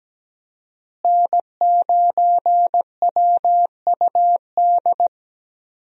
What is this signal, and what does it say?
Morse code, heard as a keyed tone, about 17 wpm, 700 Hz. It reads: N9WUD
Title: USA Call Signs - 5 Characters - Repeated 3x - 17wpm